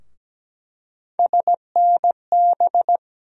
Morse code, heard as a keyed tone, about 17 wpm, 700 Hz. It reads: SNB
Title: USA Call Sign Postfixes - 3 Letters 17wpm